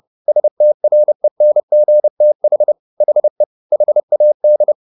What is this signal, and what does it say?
Morse code, heard as a keyed tone, about 30 wpm, 600 Hz. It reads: STRENGTH HE HAD